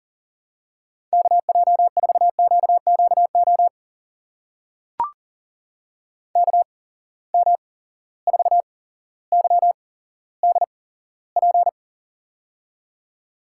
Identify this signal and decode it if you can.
Morse code, heard as a keyed tone, about 42 wpm, spaced as for 17 wpm, 700 Hz. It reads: KJ4QQO KM4YDP
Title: USA Call Signs - 6 Characters - 8x Character Spacing - 40wpm